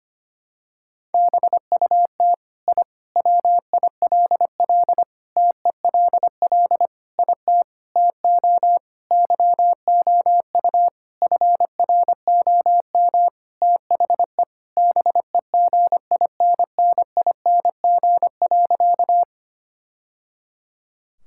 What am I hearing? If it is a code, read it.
Morse code, heard as a keyed tone, about 25 wpm, 700 Hz. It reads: BUT I WILL TELL IT TO YOU FROM THE BEGINNING.
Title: Sentences from Top 1000 Words - Review Part 5 - 25wpm